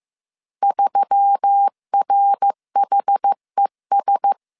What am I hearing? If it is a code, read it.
Morse code, heard as a keyed tone, about 15 wpm, 800 Hz. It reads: 3RHES